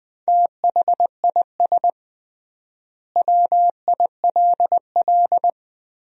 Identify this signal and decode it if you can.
Morse code, heard as a keyed tone, about 20 wpm, 700 Hz. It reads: THIS WILL
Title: 3x Word-Spacing - Sentences from Top 400 Words - 20wpm